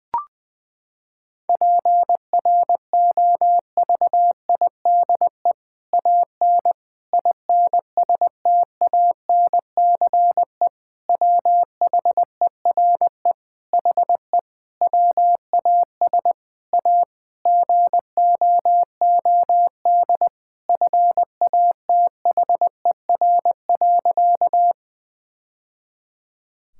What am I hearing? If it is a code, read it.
Morse code, heard as a keyed tone, about 21 wpm, 700 Hz. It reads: PROVIDE AN INSTANCE WHERE HE WAS A GOOD FATHER.
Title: Sentences from Top 1000 Words - Review Part 5 - 20wpm